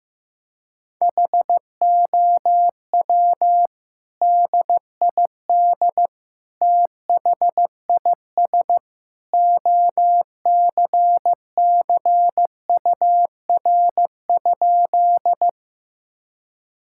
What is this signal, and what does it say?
Morse code, heard as a keyed tone, about 15 wpm, 700 Hz. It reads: HOW DID THIS OCCUR?
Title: Sentences from Top 1000 Words - Review Part 6 - 15wpm